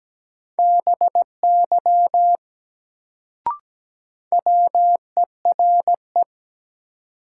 Morse code, heard as a keyed tone, about 17 wpm, 700 Hz. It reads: BY WERE
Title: Sentences from Top 100 Words - New Word Drill - 17wpm